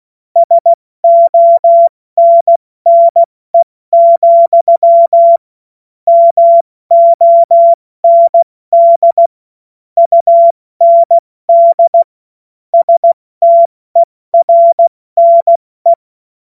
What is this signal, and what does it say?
Morse code, heard as a keyed tone, about 16 wpm, 680 Hz. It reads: SONNE, MOND UND STERNE
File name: Morsekurs / Witze / Witz 21.November 2023 70 F80